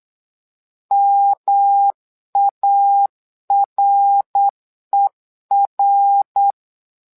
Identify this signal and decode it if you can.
Morse code, heard as a keyed tone, about 8 wpm, 800 Hz. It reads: MARER
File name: LEZIONI / 3 MWRO / 3 MWRO 8wpm 2606